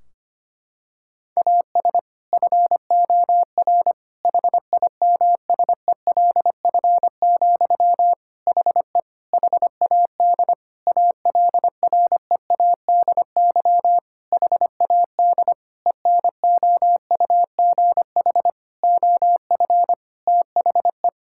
Morse code, heard as a keyed tone, about 25 wpm, 700 Hz. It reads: AS FOR HIMSELF, HE HAD ALREADY HAD ENOUGH OF THE
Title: Sentences from Top 1600 Words 25wpm